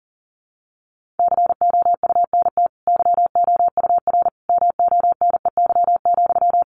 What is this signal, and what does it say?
Morse code, heard as a keyed tone, about 40 wpm, 700 Hz. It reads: COUNT YOUR MONEY,
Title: Sentences from Top 1500 Words 40wpm